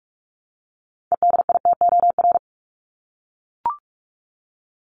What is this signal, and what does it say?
Morse code, heard as a keyed tone, about 44 wpm, 700 Hz. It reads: EDITOR
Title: Top 1501-1750 Words 45wpm